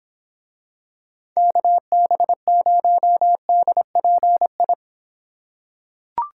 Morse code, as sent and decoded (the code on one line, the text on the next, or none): -.- -... ----- -.. .--. ..
KB0DPI